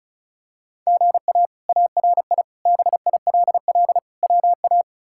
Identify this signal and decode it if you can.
Morse code, heard as a keyed tone, about 36 wpm, 700 Hz. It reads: GA ARI BILL WA
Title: K1USN SST QSO Practice - 35wpm